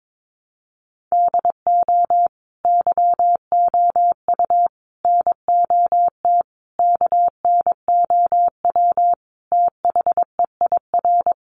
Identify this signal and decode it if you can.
Morse code, heard as a keyed tone, about 22 wpm, 700 Hz. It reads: DO YOU NOT KNOW THEIR
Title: Sentences from Top 600 Words 22wpm